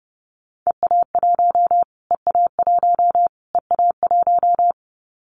Morse code, heard as a keyed tone, about 30 wpm, 700 Hz. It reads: EA1 EA1 EA1
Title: Callsign Prefixes for Top 25 DXCC - Repeated 3x - 30wpm